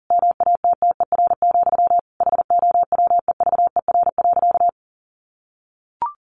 Morse code, heard as a keyed tone, about 40 wpm, 700 Hz. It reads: MATTER, HOWEVER.
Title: Sentences from Top 700 Words 40wpm